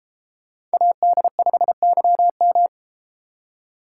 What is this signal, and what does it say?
Morse code, heard as a keyed tone, about 32 wpm, 690 Hz. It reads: AD5YM